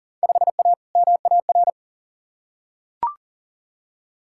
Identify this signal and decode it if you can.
Morse code, heard as a keyed tone, about 39 wpm, 700 Hz. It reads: 5A MAR
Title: Field Day Exchanges Encore - 40wpm